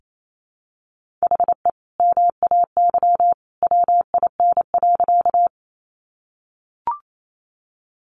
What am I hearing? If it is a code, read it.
Morse code, heard as a keyed tone, about 28 wpm, 700 Hz. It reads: HE MAY WIN.
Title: Sentences from Top 1500 Words 28wpm